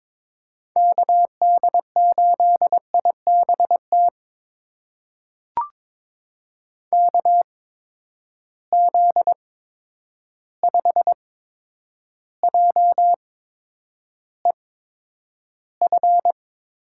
Morse code, heard as a keyed tone, about 22 wpm, spaced as for 9 wpm, 700 Hz. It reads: KD8IBT KZ5JEF
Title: USA Call Signs - 6 Characters - 8x Character Spacing - 22wpm